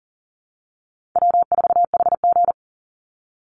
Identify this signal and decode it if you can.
Morse code, heard as a keyed tone, about 40 wpm, 700 Hz. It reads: W4HG
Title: Call Signs Repeated 3x - 40wpm